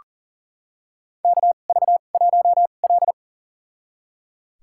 Morse code, heard as a keyed tone, about 40 wpm, spaced as for 29 wpm, 700 Hz. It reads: KV1L
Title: USA Call Signs - 4 Characters - 2x Character Spacing - 40wpm